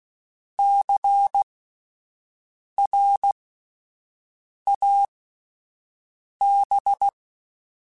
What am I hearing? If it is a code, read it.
Morse code, heard as a keyed tone, about 16 wpm, spaced as for 5 wpm, 800 Hz. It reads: CRAB
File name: Alphabetical / Alphabetical mp3s / Cs @5